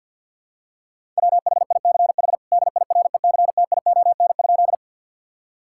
Morse code, heard as a keyed tone, about 52 wpm, 700 Hz. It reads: WHICH DIRECTION?
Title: Sentences from Top 1000 Words - Review Part 5 - 50wpm